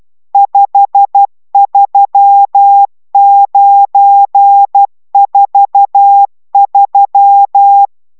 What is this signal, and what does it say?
Morse code, heard as a keyed tone, about 12 wpm, 800 Hz. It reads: 53943